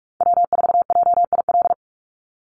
Morse code, heard as a keyed tone, about 46 wpm, 700 Hz. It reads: W4JIL
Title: USA Call Signs 45wpm